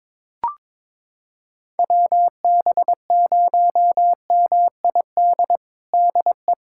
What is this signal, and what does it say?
Morse code, heard as a keyed tone, about 22 wpm, 700 Hz. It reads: WB0MID DE
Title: USA Callsign DE Callsign - 6 Characters - 22wpm